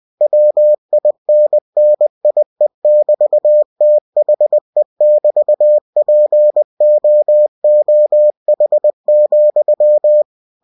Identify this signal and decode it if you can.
Morse code, heard as a keyed tone, about 20 wpm, 600 Hz. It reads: WINNIE<BT>THE<BT>POOH,